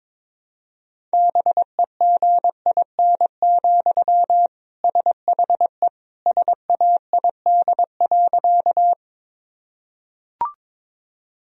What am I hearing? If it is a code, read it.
Morse code, heard as a keyed tone, about 22 wpm, 700 Hz. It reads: BEGIN, SHE SAID.
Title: Sentences from Top 1000 Words - Review Part 1 - 22wpm